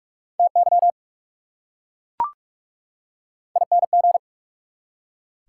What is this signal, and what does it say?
Morse code, heard as a keyed tone, about 44 wpm, 690 Hz. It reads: TY ING
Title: Prefixes and Suffixes 45wpm